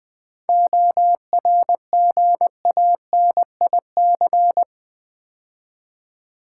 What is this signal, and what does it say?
Morse code, heard as a keyed tone, about 20 wpm, 700 Hz. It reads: ORGANIC
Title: 7 Letter Words - First 3 Letters Given - 20wpm